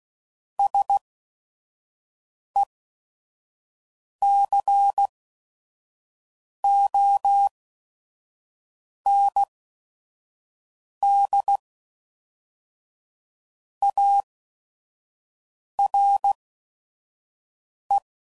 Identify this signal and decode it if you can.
Morse code, heard as a keyed tone, about 16 wpm, spaced as for 5 wpm, 800 Hz. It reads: SECOND ARE
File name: CW Code Groups @5WPM / Technical Subjects / XC